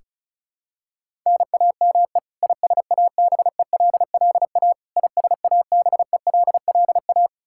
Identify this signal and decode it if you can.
Morse code, heard as a keyed tone, about 34 wpm, 700 Hz. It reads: NAME ISABELLA ISABELLA
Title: K1USN SST QSO Practice 35wpm